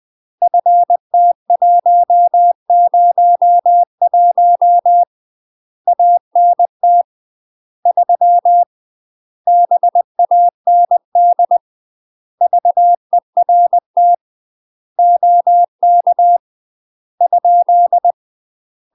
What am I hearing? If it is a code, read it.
Morse code, heard as a keyed tone, about 20 wpm, 700 Hz. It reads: FT101 ANT 3 BAND VERT OK ?
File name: UK MASTER QSOs TRAINING / 20WPM / UK MASTER QSOs 0050